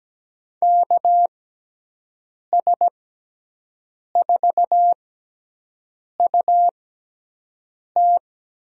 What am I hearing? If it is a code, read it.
Morse code, heard as a keyed tone, about 17 wpm, spaced as for 6 wpm, 700 Hz. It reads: KS4UT
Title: USA Call Signs - 5 Characters - 6x Character Spacing - 17wpm